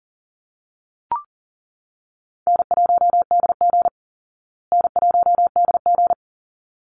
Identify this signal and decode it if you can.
Morse code, heard as a keyed tone, about 40 wpm, 700 Hz. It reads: N1DG N1DG
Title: Call Signs Repeated 3x - 40wpm